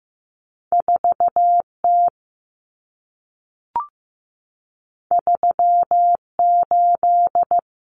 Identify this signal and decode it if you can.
Morse code, heard as a keyed tone, about 15 wpm, 700 Hz. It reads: 4T 38
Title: Common Cut Numbers - 2-Digit 15wpm